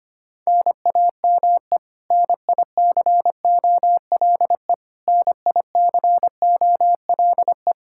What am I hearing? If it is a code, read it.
Morse code, heard as a keyed tone, about 25 wpm, 700 Hz. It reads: NAME NICOLE NICOLE
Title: Practice QSO - Exchange 1 - Beginning - 25wpm